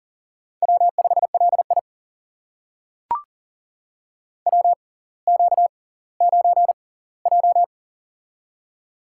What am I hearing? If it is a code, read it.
Morse code, heard as a keyed tone, about 38 wpm, spaced as for 20 wpm, 700 Hz. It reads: W5LI WQ9J
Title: USA Call Signs - 6x Character Spacing - 40wpm